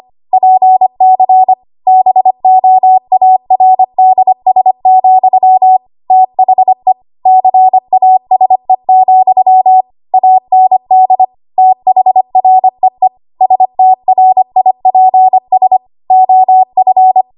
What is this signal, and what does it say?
Morse code, heard as a keyed tone, about 25 wpm, 750 Hz. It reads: PC BOARDS, THE CASE, AND THREE STRIPS OF